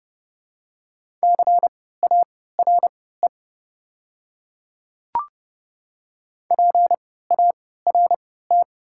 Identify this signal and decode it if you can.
Morse code, heard as a keyed tone, about 31 wpm, spaced as for 17 wpm, 700 Hz. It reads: CARE PART
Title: 3x Character Spacing - Top 300 Words - 30wpm